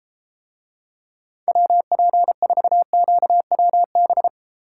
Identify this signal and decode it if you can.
Morse code, heard as a keyed tone, about 32 wpm, 700 Hz. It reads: WP4QWB